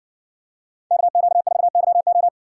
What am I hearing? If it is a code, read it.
Morse code, heard as a keyed tone, about 59 wpm, 700 Hz. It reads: KQ4YO